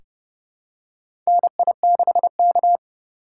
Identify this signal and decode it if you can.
Morse code, heard as a keyed tone, about 29 wpm, 700 Hz. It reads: NI6K
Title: USA Call Signs - 4 Characters - 3x Character Spacing - 30wpm